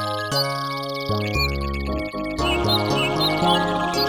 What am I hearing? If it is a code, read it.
There is no code.